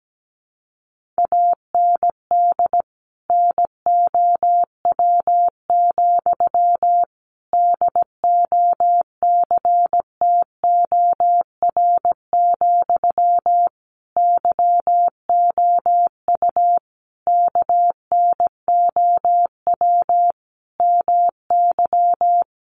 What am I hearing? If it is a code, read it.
Morse code, heard as a keyed tone, about 17 wpm, 700 Hz. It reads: AND NOW, DOCTOR, YOU KNOW MY
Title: Sentences from Top 1500 Words 17wpm